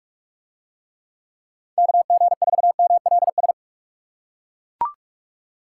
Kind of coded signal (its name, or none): Morse code